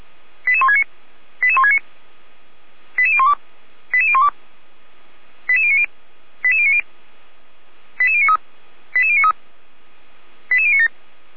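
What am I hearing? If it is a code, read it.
Five-tone selective call (ZVEI), heard as quick runs of five tones, tones of about 70 ms. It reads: 88169 88169 88012 88012 88090 88090 88093 88093 88097
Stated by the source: Received by SDR as a ZVEI five-tone selective call.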